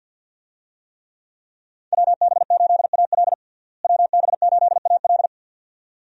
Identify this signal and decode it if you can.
Morse code, heard as a keyed tone, about 49 wpm, 700 Hz. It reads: WB8AL WB8AL
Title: USA Call Signs - 5 Characters - Repeated 2x - 50wpm